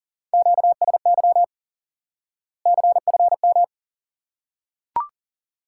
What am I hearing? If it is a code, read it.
Morse code, heard as a keyed tone, about 40 wpm, 700 Hz. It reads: QSY CFM
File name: IQE Course 3 - Lesson 89 - YAGI - Two Elements - 40wpm